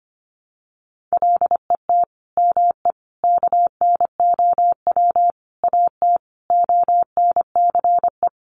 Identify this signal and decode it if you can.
Morse code, heard as a keyed tone, about 25 wpm, 700 Hz. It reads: LET ME KNOW AT ONCE
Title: Sentences from Top 300 Words 25wpm